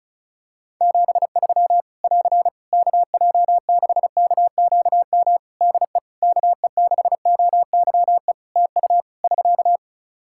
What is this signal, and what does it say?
Morse code, heard as a keyed tone, about 34 wpm, 700 Hz. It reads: 73 <AR> KJ6KQM DE KE6OYE TU <SK>
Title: Practice QSO - Exchange 4 - Ending - 35wpm